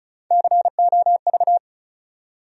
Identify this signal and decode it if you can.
Morse code, heard as a keyed tone, about 36 wpm, 700 Hz. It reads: COV
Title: USA Call Sign Postfixes - 3 Letters 35wpm